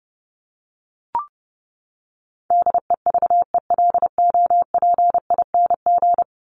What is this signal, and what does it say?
Morse code, heard as a keyed tone, about 29 wpm, 700 Hz. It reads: DEVELOPING
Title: ING Suffix - Top 100 Words with Base Words 30wpm